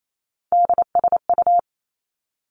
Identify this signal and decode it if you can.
Morse code, heard as a keyed tone, about 28 wpm, 700 Hz. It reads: DSU
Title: Three Challenging Letter-Number Combinations 28wpm